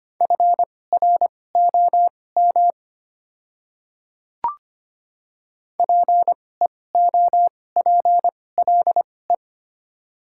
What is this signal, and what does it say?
Morse code, heard as a keyed tone, about 25 wpm, spaced as for 18 wpm, 700 Hz. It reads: FROM PEOPLE
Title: 2x Character Spacing - Top 100 Words - 25wpm